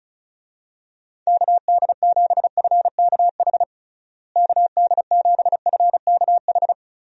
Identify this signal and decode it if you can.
Morse code, heard as a keyed tone, about 34 wpm, 700 Hz. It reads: KD7FKH KD7FKH
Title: USA Call Signs - 6 Characters - Repeated 2x - 35wpm